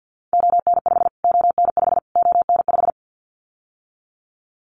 Morse code, heard as a keyed tone, about 50 wpm, 700 Hz. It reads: ON5 ON5 ON5
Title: Callsign Prefixes for Top 25 DXCC - Repeated 3x - 50wpm